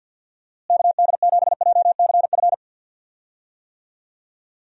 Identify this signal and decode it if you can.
Morse code, heard as a keyed tone, about 51 wpm, 700 Hz. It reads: KD7JCF
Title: USA Call Signs - 6 Characters - Repeated 2x - 50wpm